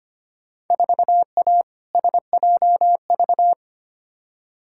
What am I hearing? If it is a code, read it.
Morse code, heard as a keyed tone, about 25 wpm, 700 Hz. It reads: 4A SJV
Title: Field Day Exchanges Encore - 25wpm